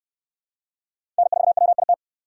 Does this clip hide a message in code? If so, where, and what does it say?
Yes: Morse code from 1.2 to 1.9 s, heard as a keyed tone, about 63 wpm, 700 Hz. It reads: N4WIT